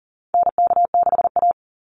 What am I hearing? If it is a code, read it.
Morse code, heard as a keyed tone, about 39 wpm, 700 Hz. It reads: NK6A
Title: NA SOTA Chaser Callsigns 40wpm